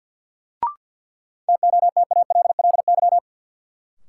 Morse code, heard as a keyed tone, about 52 wpm, 700 Hz. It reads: TOTALLY
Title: Sentences from Top 1600 Words - New Word Drill Alt - 50wpm